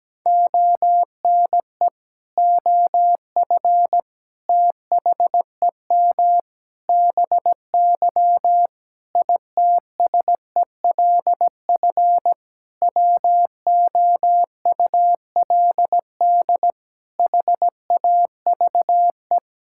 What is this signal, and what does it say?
Morse code, heard as a keyed tone, about 17 wpm, 700 Hz. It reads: ONE OF THEM BY ITSELF WOULD HAVE
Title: Sentences from Top 1000 Words - Review Part 5 - 17wpm